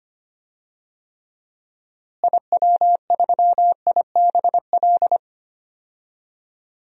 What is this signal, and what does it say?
Morse code, heard as a keyed tone, about 25 wpm, 700 Hz. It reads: IW3IBL
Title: DXCC-100 Call Signs - 6 Characters - Repeated 3x - 25wpm